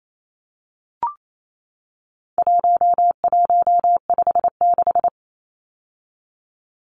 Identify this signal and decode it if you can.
Morse code, heard as a keyed tone, about 28 wpm, 700 Hz. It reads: 1156
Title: Numbers - 4-Digit 28wpm